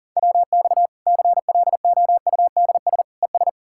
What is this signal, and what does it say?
Morse code, heard as a keyed tone, about 40 wpm, 700 Hz. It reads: WX CLOUDS ES
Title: Practice QSO - Exchange 2 - Info - 40wpm